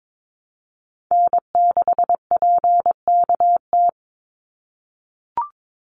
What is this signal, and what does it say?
Morse code, heard as a keyed tone, about 22 wpm, 700 Hz. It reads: N6PKT
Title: NA SOTA Chaser Callsigns 22wpm